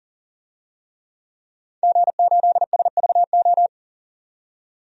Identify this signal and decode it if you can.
Morse code, heard as a keyed tone, about 40 wpm, 700 Hz. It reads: G8SVO
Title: DXCC-100 Call Signs - 5 Characters - 40wpm